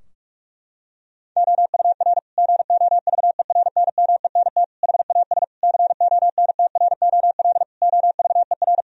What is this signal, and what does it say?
Morse code, heard as a keyed tone, about 41 wpm, 700 Hz. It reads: OUR GOVERNMENT HAS CONTROL OVER
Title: Sentences from Top 1000 Words - Review Part 4 - 45wpm